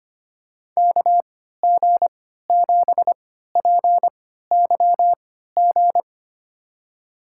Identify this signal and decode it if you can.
Morse code, heard as a keyed tone, about 25 wpm, spaced as for 14 wpm, 700 Hz. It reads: KG7PYG
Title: USA Call Signs - 3x Character Spacing - 25wpm